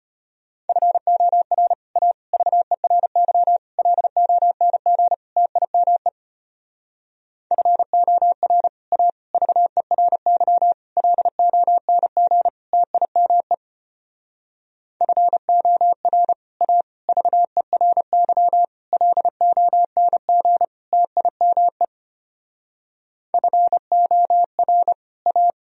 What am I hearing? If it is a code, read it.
Morse code, heard as a keyed tone, about 32 wpm, 700 Hz. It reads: FOR A VERY LONG TIME FOR A VERY LONG TIME FOR A VERY LONG TIME FOR A